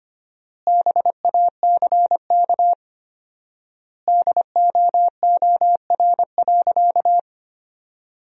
Morse code, heard as a keyed tone, about 25 wpm, 700 Hz. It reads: BACK DOOR.
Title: 4x Word-Spacing - Sentences from Top 300 Words - 25wpm